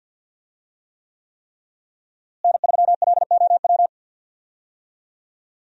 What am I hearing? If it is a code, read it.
Morse code, heard as a keyed tone, about 52 wpm, 700 Hz. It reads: N3LOW
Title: USA Call Signs - 5 Characters - Repeated 3x - 50wpm